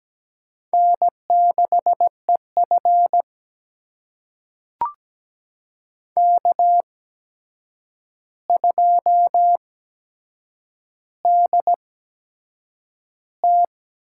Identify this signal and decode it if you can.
Morse code, heard as a keyed tone, about 17 wpm, spaced as for 7 wpm, 700 Hz. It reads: N6EF K2DT
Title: USA Call Signs - 8x Character Spacing - 17wpm